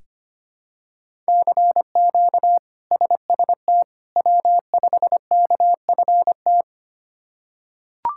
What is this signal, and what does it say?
Morse code, heard as a keyed tone, about 25 wpm, 700 Hz. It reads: CQ SST W5KFT
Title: K1USN SST QSO Practice - 25wpm